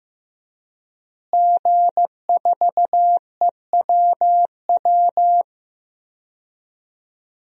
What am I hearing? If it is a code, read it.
Morse code, heard as a keyed tone, about 15 wpm, 700 Hz. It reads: G4EWW
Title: DXCC-100 Call Signs - 5 Characters - Repeated 2x - 15wpm